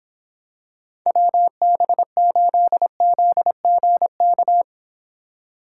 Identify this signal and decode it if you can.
Morse code, heard as a keyed tone, about 24 wpm, 700 Hz. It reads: WB8ZGK